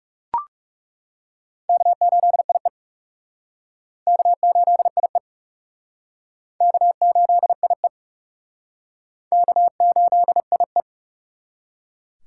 Morse code, heard as a keyed tone, about 37 wpm, 700 Hz. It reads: K8IE K8IE K8IE K8IE
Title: Speed Racing - USA 1x2 Call Signs 30wpm